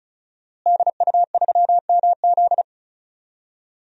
Morse code, heard as a keyed tone, about 36 wpm, 700 Hz. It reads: DU3MZ
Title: DXCC-100 Call Signs - 5 Characters - Repeated 2x - 35wpm